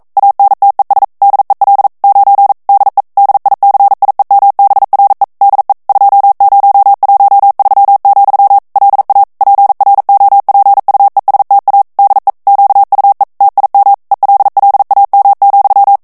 Morse code, heard as a keyed tone, about 42 wpm, 800 Hz. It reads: ANTES DEL 9 DE DICIEMBRE DE 2013, LA PROPUESTA DE QUE TIM ELLAM,
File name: morse mp3 / ESPAGNOL TEXTE EN CLAIR 1 VITESSE 40 MOTS MINUTES